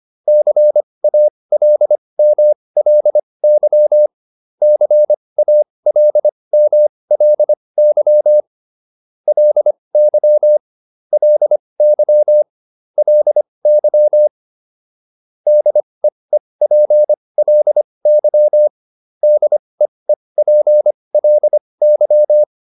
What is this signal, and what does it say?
Morse code, heard as a keyed tone, about 25 wpm, spaced as for 20 wpm, 600 Hz. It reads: CALMLY CALMLY LY LY LY DEEPLY DEEPLY
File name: LY6 20